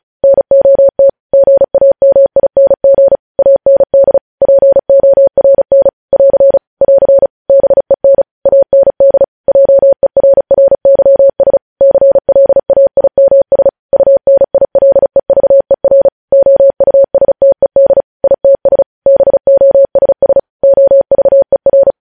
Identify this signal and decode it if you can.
Morse code, heard as a keyed tone, about 36 wpm, 550 Hz. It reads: NOT GAMING AND PORN <AR> <AR> BEN AND JERRYS CLAIMS UNILEVER OUSTED ITS BOSS OVER
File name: podcast / News headlines-1x-35x35